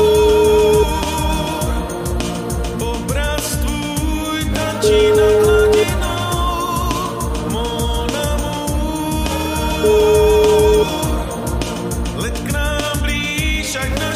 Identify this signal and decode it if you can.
Ringback tone (Europe, 425 Hz), with noise.